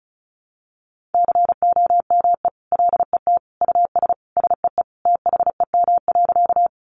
Morse code, heard as a keyed tone, about 34 wpm, 700 Hz. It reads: COME LET US SEE THEM.
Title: Sentences from Top 300 Words 35wpm